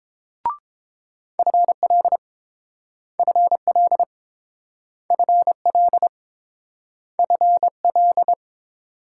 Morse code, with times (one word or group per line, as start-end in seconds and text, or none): 1.4-2.2 FL
3.2-4.0 FL
5.1-6.1 FL
7.2-8.3 FL